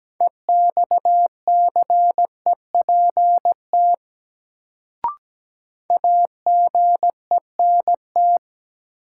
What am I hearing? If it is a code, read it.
Morse code, heard as a keyed tone, about 17 wpm, 700 Hz. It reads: EXCEPT AGENT
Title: Sentences from Top 1600 Words - New Word Drill Alt - 17wpm